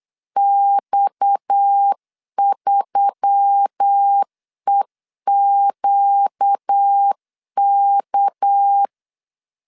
Morse code, heard as a keyed tone, about 9 wpm, 800 Hz. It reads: X3EQK
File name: CW / DET / 1 6 1500C S C 8wpm 5219